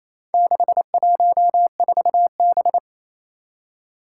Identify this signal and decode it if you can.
Morse code, heard as a keyed tone, about 28 wpm, 700 Hz. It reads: 614B
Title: Four Challenging Letter-Number Combinations 28wpm